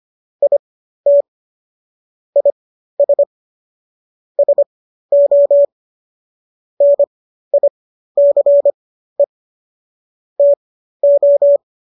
Morse code, heard as a keyed tone, about 25 wpm, spaced as for 13 wpm, 570 Hz. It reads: IT IS SO NICE TO